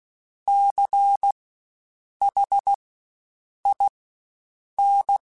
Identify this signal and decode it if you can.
Morse code, heard as a keyed tone, about 16 wpm, spaced as for 7 wpm, 800 Hz. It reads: CHIN
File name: Spaced Alphas / Spaced Alphas mp3s / Cs sp @15